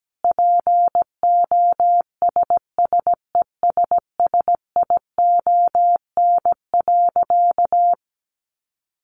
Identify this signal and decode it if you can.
Morse code, heard as a keyed tone, about 17 wpm, 700 Hz. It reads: POSSESSION.
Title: Sentences from Top 1400 Words 17wpm